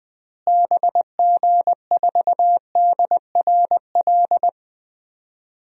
Morse code, heard as a keyed tone, about 20 wpm, 700 Hz. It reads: BG4DRL
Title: DXCC-100 Call Signs - 6 Characters - Repeated 3x - 20wpm